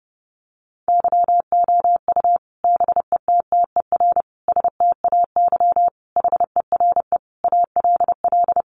Morse code, heard as a keyed tone, about 30 wpm, 700 Hz. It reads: YOU BETTER STAY HERE ALL
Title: Sentences from Top 500 Words 30wpm